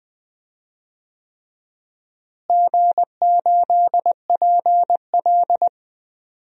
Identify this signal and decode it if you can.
Morse code, heard as a keyed tone, about 19 wpm, 700 Hz. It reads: G8PL
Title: DXCC-100 Call Signs - 4 Characters - Repeated 3x - 20wpm